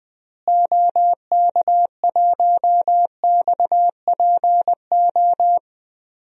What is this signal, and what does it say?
Morse code, heard as a keyed tone, about 21 wpm, 700 Hz. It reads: OK1XPO
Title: DXCC-100 Call Signs - 6 Characters - Repeated 2x - 20wpm